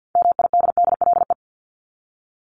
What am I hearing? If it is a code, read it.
Morse code, heard as a keyed tone, about 50 wpm, 700 Hz. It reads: MIDDLE
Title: Top 751-1000 Words 50wpm